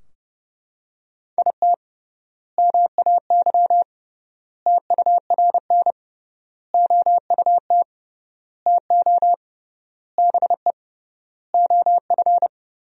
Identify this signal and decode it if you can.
Morse code, heard as a keyed tone, about 31 wpm, 700 Hz. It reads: IT MAY TURN OUT TO BE OF
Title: 3x Word-Spacing - Sentences from Top 400 Words - 30wpm